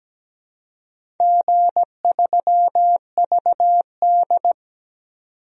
Morse code, heard as a keyed tone, about 17 wpm, 700 Hz. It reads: G3VD
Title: DXCC-100 Call Signs - 4 Characters - 17wpm